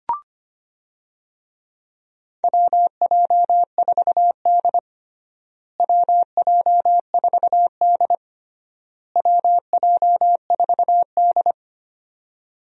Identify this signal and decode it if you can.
Morse code, heard as a keyed tone, about 25 wpm, 700 Hz. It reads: WJ4D WJ4D WJ4D
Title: USA Call Signs - 4 Characters - Repeated 3x - 25wpm